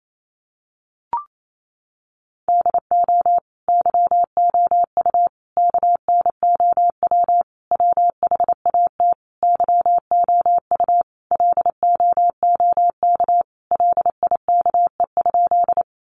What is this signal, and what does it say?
Morse code, heard as a keyed tone, about 28 wpm, 700 Hz. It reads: DO YOU KNOW WHAT YOU LOOK LIKE?
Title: Sentences from Top 100 Words 28wpm